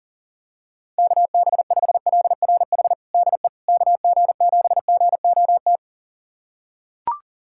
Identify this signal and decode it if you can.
Morse code, heard as a keyed tone, about 37 wpm, 700 Hz. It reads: KB5LRH DE KG7GOT
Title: USA Callsign DE Callsign 40wpm